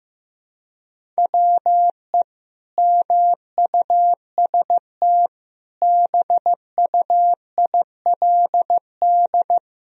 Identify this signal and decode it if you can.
Morse code, heard as a keyed tone, about 15 wpm, 700 Hz. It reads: WE MUST BUILD
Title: Sentences from Top 1000 Words - Review Part 1 - 15wpm